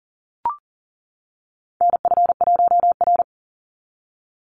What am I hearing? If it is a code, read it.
Morse code, heard as a keyed tone, about 40 wpm, 700 Hz. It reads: NF1R
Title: NA SOTA Chaser Callsigns 40wpm